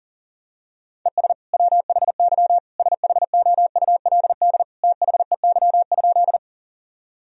Morse code, heard as a keyed tone, about 40 wpm, 700 Hz. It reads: ES WHY SHOULD THEY?
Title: Sentences from Top 300 Words and Abbreviations 40wpm